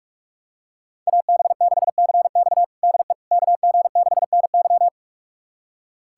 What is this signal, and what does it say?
Morse code, heard as a keyed tone, about 46 wpm, 700 Hz. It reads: AB6CX DE KG6NY
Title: USA Callsign DE Callsign - 5 Characters - 45wpm